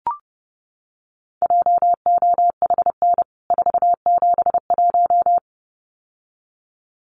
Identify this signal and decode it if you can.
Morse code, heard as a keyed tone, about 30 wpm, 700 Hz. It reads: JOHN 471
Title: CWT Exchange 30wpm